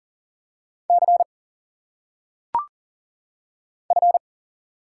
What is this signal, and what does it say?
Morse code, heard as a keyed tone, about 40 wpm, 700 Hz. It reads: C F
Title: Single Letters - Phonetic Pronunciation 40wpm